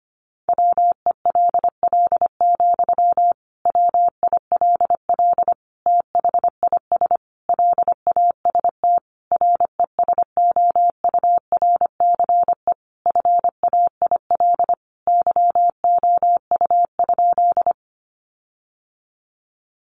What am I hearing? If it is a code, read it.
Morse code, heard as a keyed tone, about 25 wpm, 700 Hz. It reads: WELL, WILL THIS LAST RESOURCE FAIL YOU?
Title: Sentences from Top 1300 Words 25wpm